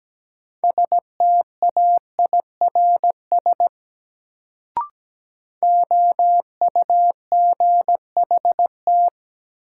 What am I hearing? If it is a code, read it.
Morse code, heard as a keyed tone, about 17 wpm, 700 Hz. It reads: STAIRS OUGHT
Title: Sets of 5 Words Encore - New Word Drill - 17wpm